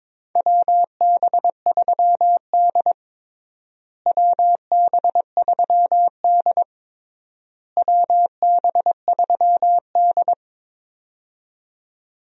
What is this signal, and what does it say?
Morse code, heard as a keyed tone, about 22 wpm, 700 Hz. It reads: WB3D WB3D WB3D
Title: USA Call Signs - 4 Characters - Repeated 3x - 22wpm